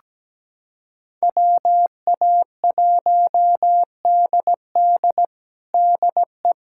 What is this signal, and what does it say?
Morse code, heard as a keyed tone, about 17 wpm, 700 Hz. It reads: WA1DD DE
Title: USA Callsign DE Callsign - 5 Characters - 17wpm